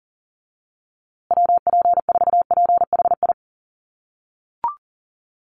Morse code, heard as a keyed tone, about 40 wpm, 700 Hz. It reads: WP4PHI
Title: USA Call Signs Repeated 3x 40wpm